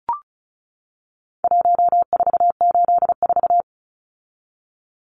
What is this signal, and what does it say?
Morse code, heard as a keyed tone, about 34 wpm, 700 Hz. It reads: 1484
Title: Numbers - Contest Style - 4-Digit 35wpm